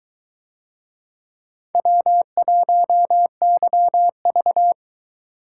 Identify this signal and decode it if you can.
Morse code, heard as a keyed tone, about 22 wpm, 700 Hz. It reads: W1YV